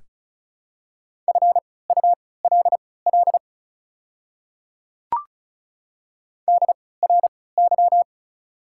Morse code, heard as a keyed tone, about 34 wpm, spaced as for 20 wpm, 700 Hz. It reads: FULL DRY
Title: 3x Character Spacing - Top 500 Words - 35wpm